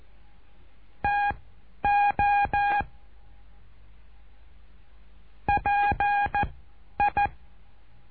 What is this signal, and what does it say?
Morse code, heard as a keyed tone, about 14 wpm, spaced as for 10 wpm, 820 Hz. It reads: TO PI
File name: cursos / CURSO DE TELEGRAFIA / Telegrafia - Trilha 08